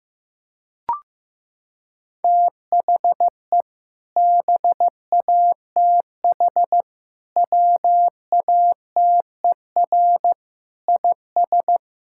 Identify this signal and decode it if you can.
Morse code, heard as a keyed tone, about 15 wpm, 700 Hz. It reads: THE BATH WATER IS
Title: Sentences from Top 1600 Words 15wpm